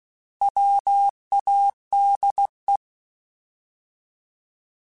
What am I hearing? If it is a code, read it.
Morse code, heard as a keyed tone, about 16 wpm, 800 Hz. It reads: WADE
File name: Alphabetical / Alphabetical mp3s / Ws @15